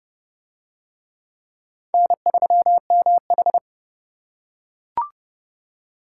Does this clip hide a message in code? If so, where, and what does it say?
Yes: Morse code from 1.9 to 3.6 s, heard as a keyed tone, about 25 wpm, 700 Hz. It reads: N3MH